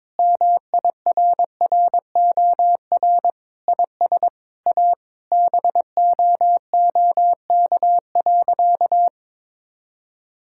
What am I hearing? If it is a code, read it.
Morse code, heard as a keyed tone, about 21 wpm, 700 Hz. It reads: MIRROR IS A BOOK.
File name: practice / 22wpm / Sentences with 4-Word Phrases - 22wpm